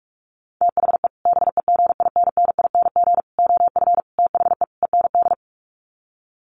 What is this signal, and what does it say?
Morse code, heard as a keyed tone, about 44 wpm, 700 Hz. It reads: THE BEGINNING OF THE END
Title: Sets of 5 Words Encore2 45wpm